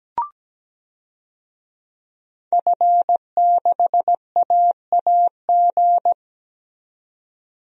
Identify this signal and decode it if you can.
Morse code, heard as a keyed tone, about 17 wpm, 700 Hz. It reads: F6AAG
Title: DXCC-100 Call Signs - 5 Characters - 17wpm